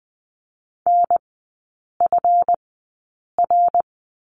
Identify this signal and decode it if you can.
Morse code, heard as a keyed tone, about 20 wpm, spaced as for 8 wpm, 700 Hz. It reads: NFR
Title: 184 Lesson 30e - Letter b - Three Characters 20wpm